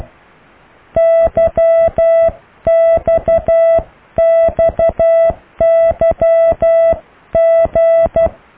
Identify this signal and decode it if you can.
Morse code, heard as a keyed tone, about 12 wpm, 660 Hz. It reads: YXXYG